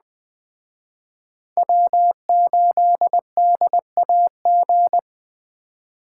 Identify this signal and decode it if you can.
Morse code, heard as a keyed tone, about 21 wpm, 700 Hz. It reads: W8DAG